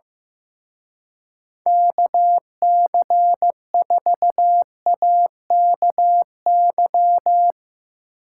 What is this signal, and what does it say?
Morse code, heard as a keyed tone, about 16 wpm, 700 Hz. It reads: KC4AKY
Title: Speed Racing - USA 2x3 Call Signs 15wpm